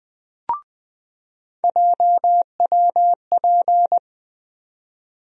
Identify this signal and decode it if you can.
Morse code, heard as a keyed tone, about 20 wpm, 700 Hz. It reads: JWP
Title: USA Call Sign Postfixes - 3 Letters 20wpm